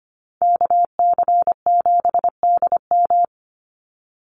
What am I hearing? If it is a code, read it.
Morse code, heard as a keyed tone, about 25 wpm, 700 Hz. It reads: KC7DM
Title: NA SOTA Activator Callsigns 25wpm